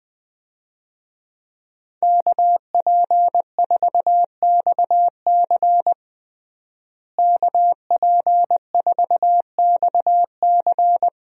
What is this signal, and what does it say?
Morse code, heard as a keyed tone, about 20 wpm, 700 Hz. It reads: KP4XC KP4XC
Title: DXCC-100 Call Signs - 5 Characters - Repeated 3x - 20wpm